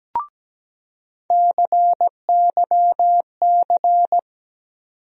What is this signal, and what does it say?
Morse code, heard as a keyed tone, about 17 wpm, 700 Hz. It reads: CYC
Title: USA Call Sign Postfixes - 3 Letters 17wpm